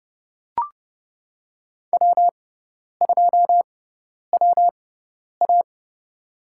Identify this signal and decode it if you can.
Morse code, heard as a keyed tone, about 32 wpm, spaced as for 10 wpm, 700 Hz. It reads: W2WA
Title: USA Call Signs - 4 Characters - 6x Character Spacing - 30wpm